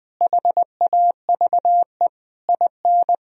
Morse code, heard as a keyed tone, about 20 wpm, 700 Hz. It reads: HAVE IN
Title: Sentences from Top 1000 Words - Review Part 3 - 20wpm